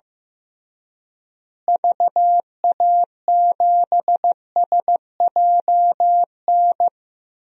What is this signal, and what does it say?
Morse code, heard as a keyed tone, about 16 wpm, 700 Hz. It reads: VA7SJN
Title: DXCC-100 Call Signs - 6 Characters - Repeated 3x - 15wpm